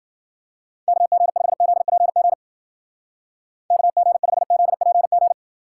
Morse code, heard as a keyed tone, about 50 wpm, 700 Hz. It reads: KG5ZPG KG5ZPG
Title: Speed Racing - USA 2x3 Call Signs 40wpm